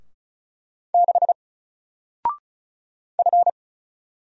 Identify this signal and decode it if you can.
Morse code, heard as a keyed tone, about 33 wpm, 700 Hz. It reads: 6 F
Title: Single Letter-Number Alt 35wpm